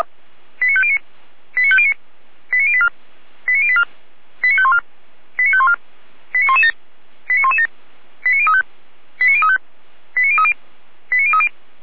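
Five-tone selective call (ZVEI), heard as quick runs of five tones, tones of about 70 ms. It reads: 89509 89509 89084 89084 89514 89514 89117 89117 89035 89035 89030 89030
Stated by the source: ZVEI five-tone selective call received on an SDR.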